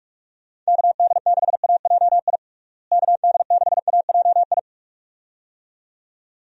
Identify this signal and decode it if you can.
Morse code, heard as a keyed tone, about 41 wpm, 700 Hz. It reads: KD6AJI KD6AJI
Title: USA Call Signs - 6 Characters - Repeated 2x - 45wpm